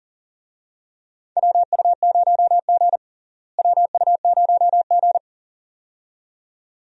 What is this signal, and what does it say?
Morse code, heard as a keyed tone, about 40 wpm, 700 Hz. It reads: WU0G WU0G
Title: USA Call Signs - 4 Characters - Repeated 2x - 40wpm